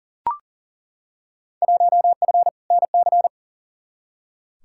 Morse code, heard as a keyed tone, about 40 wpm, 700 Hz. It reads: 1F NC
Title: Field Day Exchanges - 40wpm